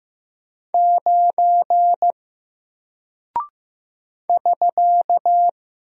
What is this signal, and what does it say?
Morse code, heard as a keyed tone, about 15 wpm, 700 Hz. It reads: 9 <SK>
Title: Single Character (Letter:Number:Punctuation:Prosign) Alt 15wpm